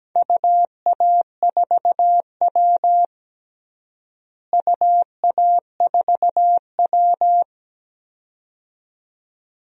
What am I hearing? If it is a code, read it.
Morse code, heard as a keyed tone, about 17 wpm, 700 Hz. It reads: UA4W UA4W
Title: DXCC-100 Call Signs - 4 Characters - Repeated 2x - 17wpm